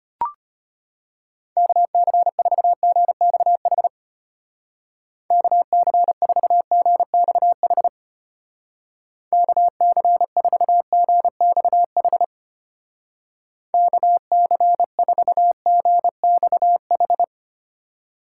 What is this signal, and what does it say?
Morse code, heard as a keyed tone, about 32 wpm, 700 Hz. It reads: KC4GXH KC4GXH KC4GXH KC4GXH